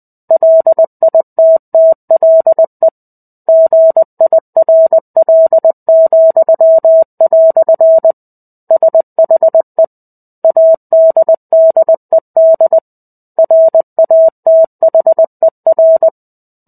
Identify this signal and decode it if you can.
Morse code, heard as a keyed tone, about 20 wpm, 650 Hz. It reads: LITTLE GIRL," SHE ADDED RATHER
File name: 05-Chapter-05 20 WPM